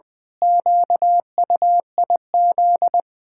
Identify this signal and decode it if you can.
Morse code, heard as a keyed tone, about 21 wpm, 700 Hz. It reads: QUIZ
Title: Pangram Challenge v2 - 20wpm